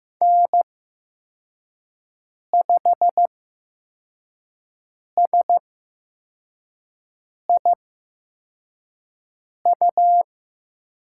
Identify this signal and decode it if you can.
Morse code, heard as a keyed tone, about 15 wpm, spaced as for 4 wpm, 700 Hz. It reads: N5SIU
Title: USA Call Signs - 5 Characters - 8x Character Spacing - 15wpm